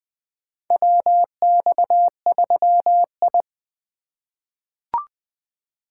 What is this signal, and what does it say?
Morse code, heard as a keyed tone, about 20 wpm, 700 Hz. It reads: WX3I
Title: USA Call Signs - 3x Character Spacing - 20wpm